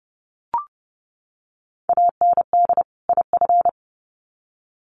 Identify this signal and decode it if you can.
Morse code, heard as a keyed tone, about 30 wpm, 700 Hz. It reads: AND IF
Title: Sets of 2 Words 30wpm